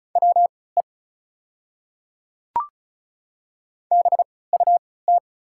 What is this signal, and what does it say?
Morse code, heard as a keyed tone, about 33 wpm, spaced as for 20 wpm, 690 Hz. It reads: WE BUT
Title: 3x Character Spacing - Top 100 Words - 35wpm